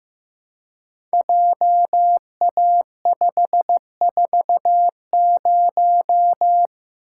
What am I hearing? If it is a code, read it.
Morse code, heard as a keyed tone, about 15 wpm, 700 Hz. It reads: JA540
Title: Code Groups - 5 Letter:Numbers - 15wpm